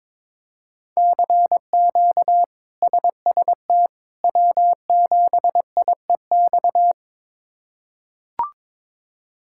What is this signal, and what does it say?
Morse code, heard as a keyed tone, about 22 wpm, 700 Hz. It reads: CQ SST W7IEX
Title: K1USN SST QSO Practice - 22wpm